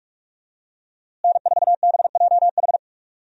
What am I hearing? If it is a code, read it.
Morse code, heard as a keyed tone, about 46 wpm, 700 Hz. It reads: N4BJH
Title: USA Call Signs - 5 Characters - Repeated 3x - 45wpm